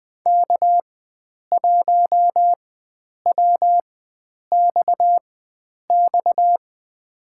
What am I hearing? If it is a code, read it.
Morse code, heard as a keyed tone, about 20 wpm, spaced as for 9 wpm, 700 Hz. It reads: K1WXX
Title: USA Call Signs - 5 Characters - 4x Character Spacing - 20wpm